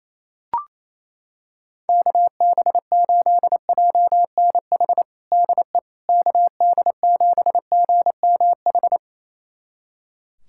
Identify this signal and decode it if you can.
Morse code, heard as a keyed tone, about 28 wpm, 700 Hz. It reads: KB8JNH DE KD7GMH
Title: USA Callsign DE Callsign - 6 Characters - 28wpm